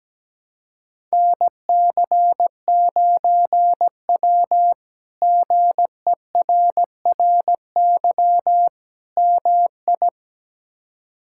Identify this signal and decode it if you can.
Morse code, heard as a keyed tone, about 17 wpm, 700 Hz. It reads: NC9W GERRY MI
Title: K1USN SST QSO Practice - 17wpm